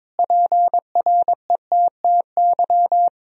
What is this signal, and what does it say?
Morse code, heard as a keyed tone, about 21 wpm, 700 Hz. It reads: PRETTY
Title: Sentences from Top 1000 Words - Review Part 3 - 22wpm